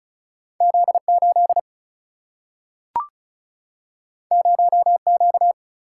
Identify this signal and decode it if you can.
Morse code, heard as a keyed tone, about 36 wpm, 700 Hz. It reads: Z8 0Q
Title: Single-Letter Single-Number 35wpm